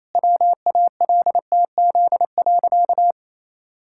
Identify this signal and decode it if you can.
Morse code, heard as a keyed tone, about 28 wpm, 700 Hz. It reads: WALTZ.
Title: Pangram Challenge v2 - 28wpm